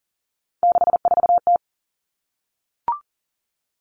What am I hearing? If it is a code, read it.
Morse code, heard as a keyed tone, about 39 wpm, 700 Hz. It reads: 64T
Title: Common Cut Numbers - 3-Digit 40wpm